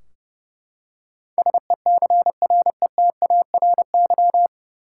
Morse code, heard as a keyed tone, about 30 wpm, 700 Hz. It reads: SECRETARY
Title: K1USN SST QSO Practice 30wpm